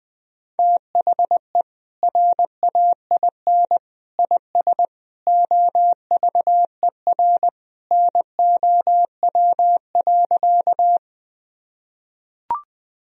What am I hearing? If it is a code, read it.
Morse code, heard as a keyed tone, about 21 wpm, 700 Hz. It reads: THE RAIN IS OVER NOW.
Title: Sentences from Top 1000 Words - Review Part 1 - 20wpm